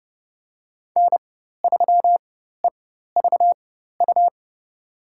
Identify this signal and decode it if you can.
Morse code, heard as a keyed tone, about 30 wpm, spaced as for 14 wpm, 700 Hz. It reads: N3EVU
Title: USA Call Signs - 5 Characters - 4x Character Spacing - 30wpm